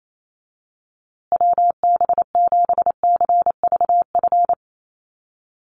Morse code, heard as a keyed tone, about 28 wpm, 700 Hz. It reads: WB7CVF
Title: USA Call Signs Repeated 2x 28wpm